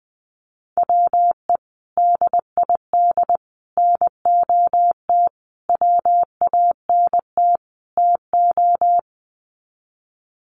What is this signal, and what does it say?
Morse code, heard as a keyed tone, about 20 wpm, 700 Hz. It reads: WE DID NOT WANT TO
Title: Sets of 5 Words Encore 20wpm